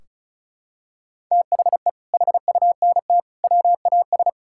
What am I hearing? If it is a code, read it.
Morse code, heard as a keyed tone, about 36 wpm, 700 Hz. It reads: THE HUNT WAS
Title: Sentences from Top 1600 Words 35wpm